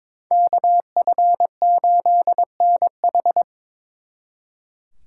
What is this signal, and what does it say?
Morse code, heard as a keyed tone, about 26 wpm, 700 Hz. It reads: KF8NH
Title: Speed Racing - USA Call Signs 22wpm